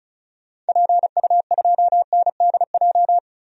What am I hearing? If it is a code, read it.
Morse code, heard as a keyed tone, about 34 wpm, 700 Hz. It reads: PU2NDJ
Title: DXCC-100 Call Signs - 6 Characters - 35wpm